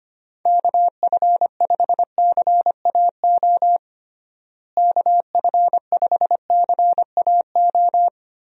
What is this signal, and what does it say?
Morse code, heard as a keyed tone, about 25 wpm, 700 Hz. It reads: KF5CAO KF5CAO
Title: USA Call Signs - 6 Characters - Repeated 3x - 25wpm